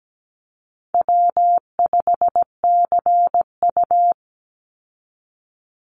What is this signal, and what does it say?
Morse code, heard as a keyed tone, about 17 wpm, 700 Hz. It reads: W5CU
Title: Call Signs Repeated 3x - 17wpm